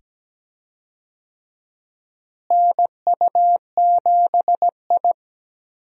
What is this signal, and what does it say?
Morse code, heard as a keyed tone, about 17 wpm, 700 Hz. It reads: NU7I
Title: USA Call Signs - 4 Characters - Repeated 2x - 17wpm